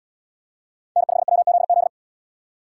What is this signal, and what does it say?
Morse code, heard as a keyed tone, about 68 wpm, 700 Hz. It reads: N5WZZ